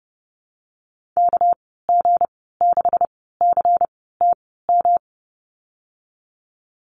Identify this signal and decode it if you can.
Morse code, heard as a keyed tone, about 30 wpm, spaced as for 17 wpm, 700 Hz. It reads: KG6CTM
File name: Lesson 37g - Letter x - Call Signs 30wpm